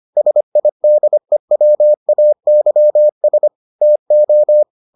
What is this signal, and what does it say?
Morse code, heard as a keyed tone, about 25 wpm, 600 Hz. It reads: SIDEWAYS TO